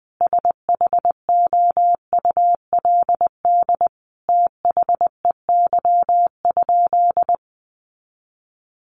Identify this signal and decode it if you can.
Morse code, heard as a keyed tone, about 20 wpm, 700 Hz. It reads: SHOULD THEY?
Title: Sentences from Top 300 Words 20wpm